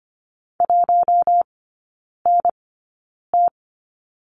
Morse code, heard as a keyed tone, about 25 wpm, spaced as for 9 wpm, 700 Hz. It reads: 1NT
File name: Lesson 15e - Number 5 - Three Characters 25wpm